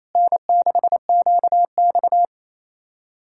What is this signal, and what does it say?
Morse code, heard as a keyed tone, about 37 wpm, 700 Hz. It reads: N6QX